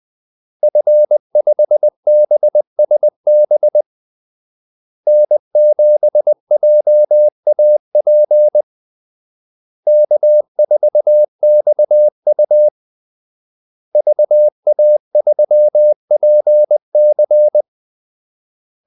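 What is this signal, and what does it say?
Morse code, heard as a keyed tone, about 20 wpm, 600 Hz. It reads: F5BSB N7JAP K4XU VA3PC
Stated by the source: Morse call-sign practice at 20 wpm.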